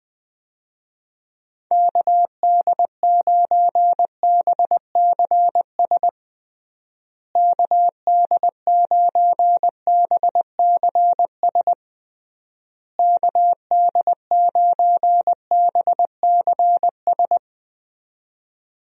Morse code, heard as a keyed tone, about 21 wpm, 700 Hz. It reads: KD9BCS KD9BCS KD9BCS
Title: USA Call Signs - 6 Characters - Repeated 3x - 20wpm